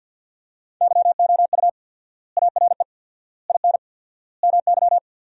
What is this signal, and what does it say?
Morse code, heard as a keyed tone, about 51 wpm, 700 Hz. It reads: YOU ARE IN MY
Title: 4x Word-Spacing - Sentences from Top 500 Words - 50wpm